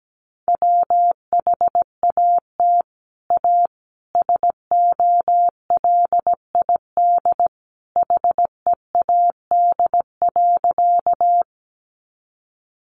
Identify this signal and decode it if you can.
Morse code, heard as a keyed tone, about 17 wpm, 700 Hz. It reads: WHAT A SOLID HEAD.
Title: Sentences from Top 1500 Words 17wpm